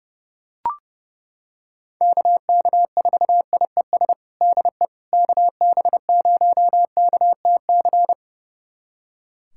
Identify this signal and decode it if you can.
Morse code, heard as a keyed tone, about 30 wpm, 700 Hz. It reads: KK4IES DE KB0KTC
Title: USA Callsign DE Callsign 30wpm